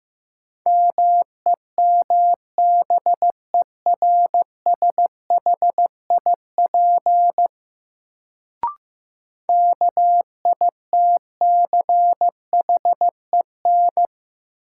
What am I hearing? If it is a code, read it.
Morse code, heard as a keyed tone, about 15 wpm, 700 Hz. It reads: MEMBERSHIP KITCHEN
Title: Sentences from Top 1400 Words - New Word Drill - 15wpm